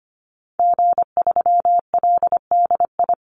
Morse code, heard as a keyed tone, about 25 wpm, 700 Hz. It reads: G3LDI
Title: Call Signs Repeated 2x - 25wpm